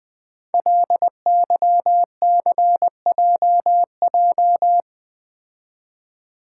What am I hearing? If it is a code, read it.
Morse code, heard as a keyed tone, about 20 wpm, 700 Hz. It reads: LYCJJ
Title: Code Groups - 5 Letter:Numbers - 20wpm